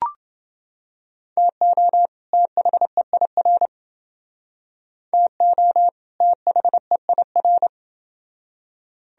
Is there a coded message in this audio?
Yes: Morse code from 1.4 to 7.7 s, heard as a keyed tone, about 28 wpm, 700 Hz. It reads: TO THEIR TO THEIR